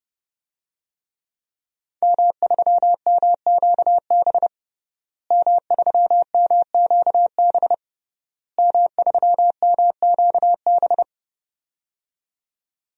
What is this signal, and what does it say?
Morse code, heard as a keyed tone, about 30 wpm, 700 Hz. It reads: M3MQB M3MQB M3MQB
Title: DXCC-100 Call Signs - 5 Characters - Repeated 3x - 30wpm